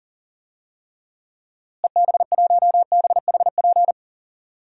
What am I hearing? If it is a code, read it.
Morse code, heard as a keyed tone, about 40 wpm, 700 Hz. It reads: EB1BHP
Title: DXCC-100 Call Signs - 6 Characters - Repeated 2x - 40wpm